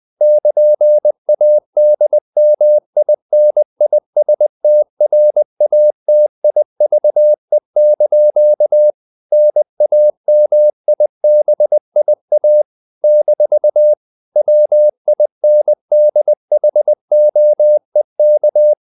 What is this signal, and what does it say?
Morse code, heard as a keyed tone, about 20 wpm, 600 Hz. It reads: <KN>ADMINISTRATIVE) NAMIBIA - WINDHOEK
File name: CW / Practice / NationalCapitals20